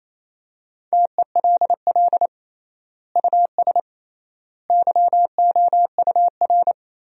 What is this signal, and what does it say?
Morse code, heard as a keyed tone, about 28 wpm, 700 Hz. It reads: TELL US YOUR
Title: 3x Word-Spacing - Sentences from Top 200 Words - 28wpm